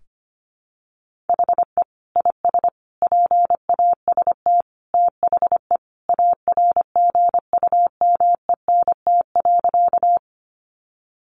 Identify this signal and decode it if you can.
Morse code, heard as a keyed tone, about 25 wpm, 700 Hz. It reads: HE IS PAST THE ARGUMENT.
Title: Sentences from Top 1400 Words 25wpm